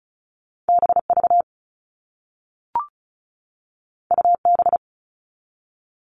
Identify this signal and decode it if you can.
Morse code, heard as a keyed tone, about 36 wpm, 700 Hz. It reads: BV UB
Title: All Cut Numbers - 2-Digit 35wpm